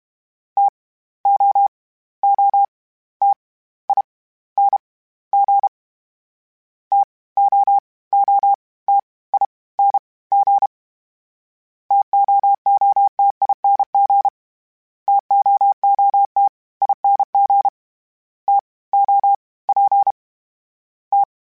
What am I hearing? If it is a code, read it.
Morse code, heard as a keyed tone, about 32 wpm, 800 Hz. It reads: T O O T I N G T O O T I N G TOOTING TOOT ING T O P T
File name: Misc / Misc-mp3s / TO @30